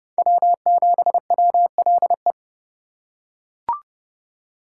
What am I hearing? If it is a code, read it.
Morse code, heard as a keyed tone, about 28 wpm, 700 Hz. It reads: W7WLE